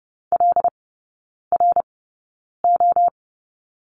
Morse code, heard as a keyed tone, about 30 wpm, spaced as for 9 wpm, 700 Hz. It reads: LRO
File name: course / 30:10wpm / 087 Lesson 15e - Number 5 - Three Characters 30wpm